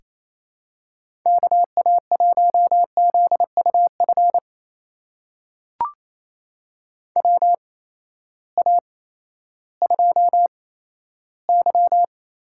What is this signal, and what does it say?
Morse code, heard as a keyed tone, about 28 wpm, spaced as for 14 wpm, 700 Hz. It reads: KA1ZUF WA2Y
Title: USA Call Signs - 6 Characters - 8x Character Spacing - 28wpm